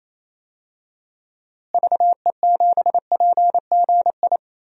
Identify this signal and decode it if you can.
Morse code, heard as a keyed tone, about 28 wpm, 700 Hz. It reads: VE7PGI
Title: DXCC-100 Call Signs - 6 Characters - Repeated 2x - 28wpm